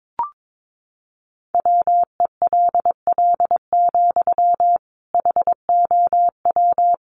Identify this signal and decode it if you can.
Morse code, heard as a keyed tone, about 22 wpm, 700 Hz. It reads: WELL, HOW
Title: Sentences from Top 1100 Words 22wpm